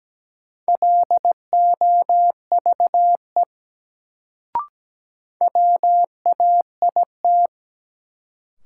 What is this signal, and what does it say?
Morse code, heard as a keyed tone, about 17 wpm, 700 Hz. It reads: LOVE WAIT
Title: Sets of 5 Words Encore - New Word Drill - 17wpm